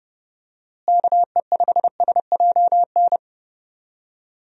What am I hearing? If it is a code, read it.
Morse code, heard as a keyed tone, about 30 wpm, 700 Hz. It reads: KE5SJN
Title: USA Call Signs - 6 Characters - Repeated 3x - 30wpm